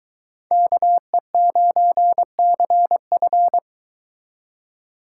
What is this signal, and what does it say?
Morse code, heard as a keyed tone, about 22 wpm, 700 Hz. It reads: KE9CF